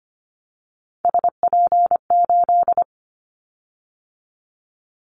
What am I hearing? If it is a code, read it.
Morse code, heard as a keyed tone, about 25 wpm, 700 Hz. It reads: SP8
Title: Callsign Prefixes for Top 25 DXCC - 25wpm